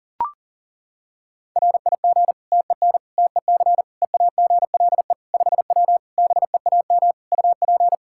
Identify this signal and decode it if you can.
Morse code, heard as a keyed tone, about 40 wpm, 700 Hz. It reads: RIG TEN TEC EAGLE 5W BEAM UP
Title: Practice QSO - Exchange 2 - Info - 40wpm